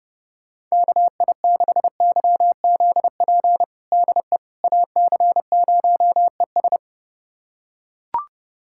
Morse code, heard as a keyed tone, about 29 wpm, 700 Hz. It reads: KI6YZP DE AC0ES
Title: USA Callsign DE Callsign 30wpm